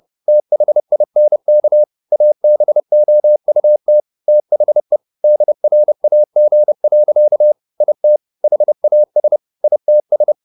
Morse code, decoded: THINK ABOUT THE DRAG. IT HAS ITS

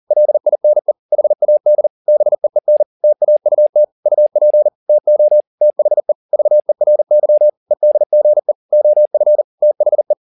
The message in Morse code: LINE HAD BEEN TAUT UP TO THE VERY EDGE OF THE

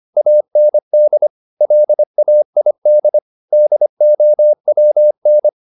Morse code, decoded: AND LAID DOWN

